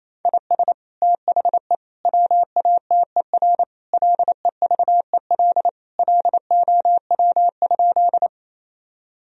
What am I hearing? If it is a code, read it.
Morse code, heard as a keyed tone, about 28 wpm, 700 Hz. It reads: IS THE WATER LEVEL LOW?
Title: Sentences from Top 1000 Words - Review Part 4 - 28wpm